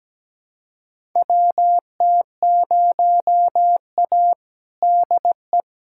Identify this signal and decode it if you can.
Morse code, heard as a keyed tone, about 17 wpm, 700 Hz. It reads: WT0A DE
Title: USA Callsign DE Callsign - 4 Characters - 17wpm